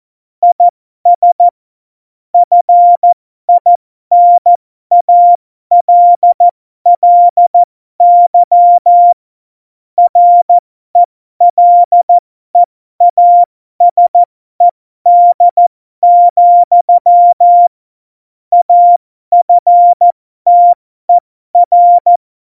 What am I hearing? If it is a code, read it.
Morse code, heard as a keyed tone, about 14 wpm, spaced as for 12 wpm, 700 Hz. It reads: IS FINALLY RELEASED, AFTER
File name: Morsekurs / Witze